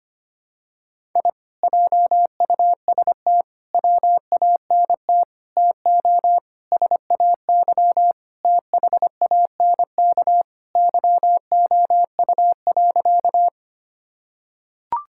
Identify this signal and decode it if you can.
Morse code, heard as a keyed tone, about 25 wpm, 700 Hz. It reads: I JUST WANT TO SAY THANK YOU.
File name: practice / 25wpm / Sentences with 2-Word Phrases - 25wpm